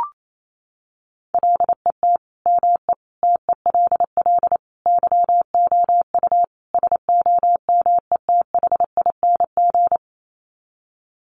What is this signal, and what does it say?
Morse code, heard as a keyed tone, about 28 wpm, 700 Hz. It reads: LET ME TELL YOU SOMETHING
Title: Sets of 5 Words 28wpm